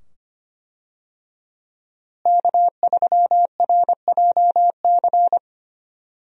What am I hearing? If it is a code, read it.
Morse code, heard as a keyed tone, about 25 wpm, 700 Hz. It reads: K3RJC
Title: USA Call Signs - 5 Characters - Repeated 2x - 25wpm